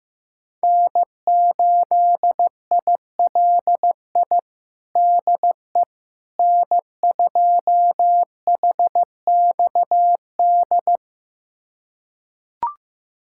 Morse code, heard as a keyed tone, about 15 wpm, 700 Hz. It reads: N8ILI DE N2HXD
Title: USA Callsign DE Callsign - 5 Characters - 15wpm